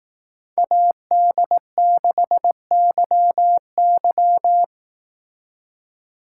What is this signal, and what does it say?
Morse code, heard as a keyed tone, about 18 wpm, 700 Hz. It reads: AD6YY
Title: Speed Racing - USA 2x2 Call Signs 15wpm